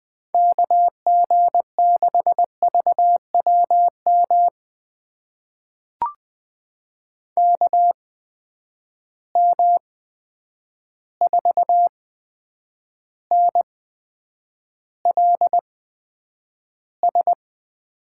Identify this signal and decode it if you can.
Morse code, heard as a keyed tone, about 20 wpm, spaced as for 9 wpm, 700 Hz. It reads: KG6VWM KM4NLS
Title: USA Call Signs - 6 Characters - 8x Character Spacing - 20wpm